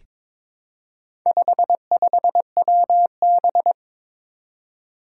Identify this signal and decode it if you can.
Morse code, heard as a keyed tone, about 23 wpm, 700 Hz. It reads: 55WB
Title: Four Challenging Letter-Number Combinations 22wpm